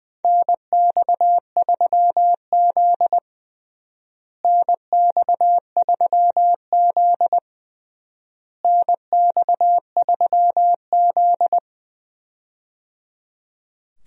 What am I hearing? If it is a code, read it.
Morse code, heard as a keyed tone, about 20 wpm, 700 Hz. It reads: NX3Z NX3Z NX3Z
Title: USA Call Signs - 4 Characters - Repeated 3x - 20wpm